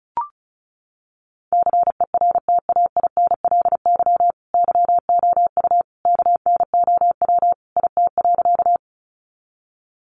Morse code, heard as a keyed tone, about 34 wpm, 700 Hz. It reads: CERTAINLY YOU KNOW IT.
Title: Sentences from Top 900 Words 35wpm